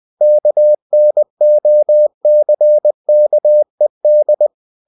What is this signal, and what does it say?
Morse code, heard as a keyed tone, about 20 wpm, 600 Hz. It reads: KNOCKED
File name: cw-books / winnie-the-pooh / winnie-20wpm-page-0048